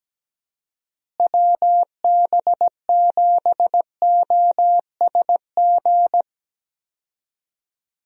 Morse code, heard as a keyed tone, about 17 wpm, 700 Hz. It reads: WB7OSG
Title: USA Call Signs - 6 Characters - Repeated 3x - 17wpm